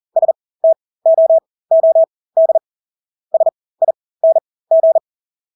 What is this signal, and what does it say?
Morse code, heard as a keyed tone, about 40 wpm, spaced as for 20 wpm, 650 Hz. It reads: STOOD SING
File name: Engelske ord 40-20 wpm